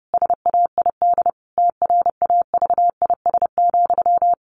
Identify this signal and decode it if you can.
Morse code, heard as a keyed tone, about 30 wpm, 700 Hz. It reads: SAID TRAVIS,
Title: A Sound Of Thunder 30wpm